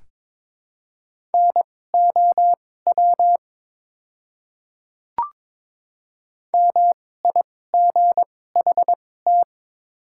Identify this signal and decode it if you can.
Morse code, heard as a keyed tone, about 22 wpm, spaced as for 16 wpm, 700 Hz. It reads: NOW MIGHT
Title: 2x Character Spacing - Top 300 Words - 22wpm